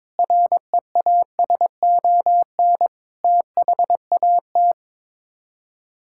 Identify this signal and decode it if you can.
Morse code, heard as a keyed tone, about 22 wpm, 700 Hz. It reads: REASON THAT
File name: practice / 22wpm / Sets of 6 Words - 22wpm